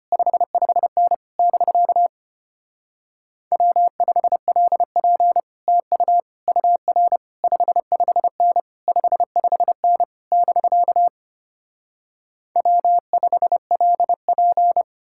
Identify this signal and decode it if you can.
Morse code, heard as a keyed tone, about 33 wpm, 700 Hz. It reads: 55N <BK> W5LP TU UR 55N 55N <BK> W5LP